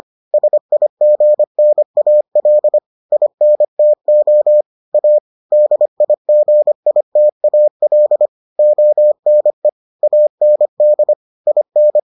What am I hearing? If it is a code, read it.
Morse code, heard as a keyed tone, about 25 wpm, 600 Hz. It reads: SIGNAL INTO A DIGITAL ONE AND IN